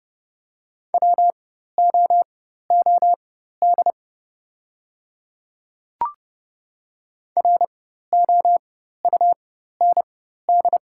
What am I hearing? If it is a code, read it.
Morse code, heard as a keyed tone, about 30 wpm, spaced as for 14 wpm, 700 Hz. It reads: WOOD ROUND
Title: 4x Character Spacing - Top 300 Words - 30wpm